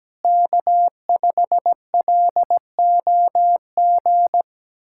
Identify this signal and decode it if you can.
Morse code, heard as a keyed tone, about 17 wpm, 700 Hz. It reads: K5LOG
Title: USA Callsign DE Callsign - 5 Characters - 17wpm